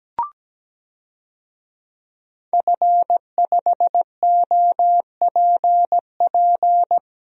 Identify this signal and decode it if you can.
Morse code, heard as a keyed tone, about 17 wpm, 700 Hz. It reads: F5OPP
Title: DXCC-100 Call Signs - 5 Characters - Repeated 2x - 17wpm